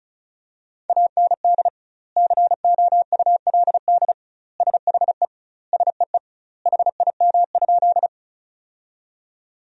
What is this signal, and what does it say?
Morse code, heard as a keyed tone, about 36 wpm, 700 Hz. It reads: AND COULD SHE SEE HIM?
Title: 2x Word-Spacing - Sentences from Top 100 Words - 35wpm